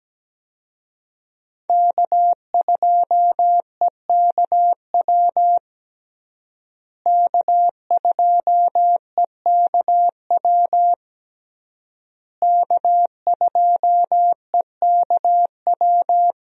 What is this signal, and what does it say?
Morse code, heard as a keyed tone, about 17 wpm, 700 Hz. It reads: K2EKW K2EKW K2EKW
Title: USA Call Signs - 5 Characters - Repeated 3x - 17wpm